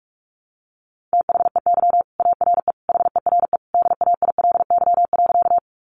Morse code, heard as a keyed tone, about 44 wpm, 700 Hz. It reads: THEY ARE HERE DAILY.
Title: Sentences from Top 1100 Words 45wpm